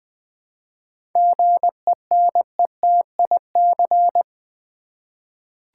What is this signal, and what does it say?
Morse code, heard as a keyed tone, about 20 wpm, 700 Hz. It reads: GENETIC
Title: 7 Letter Words - First 3 Letters Given - 20wpm